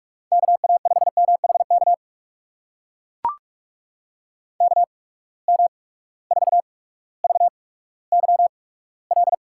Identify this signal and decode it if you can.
Morse code, heard as a keyed tone, about 43 wpm, spaced as for 19 wpm, 700 Hz. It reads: KA5MHK KM4VYL
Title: USA Call Signs - 6 Characters - 8x Character Spacing - 45wpm